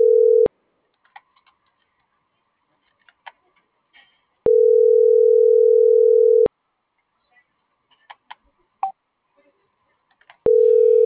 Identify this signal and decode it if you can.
Ringback tone (North America), 440 + 480 Hz.